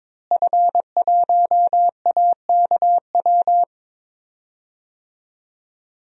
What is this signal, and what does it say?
Morse code, heard as a keyed tone, about 22 wpm, 700 Hz. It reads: F1AKW
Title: DXCC-100 Call Signs - 5 Characters - Repeated 3x - 22wpm